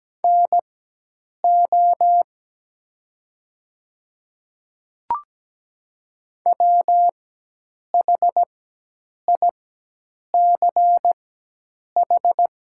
Morse code, heard as a keyed tone, about 17 wpm, spaced as for 8 wpm, 700 Hz. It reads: NO WHICH
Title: 4x Character Spacing - Top 100 Words - 17wpm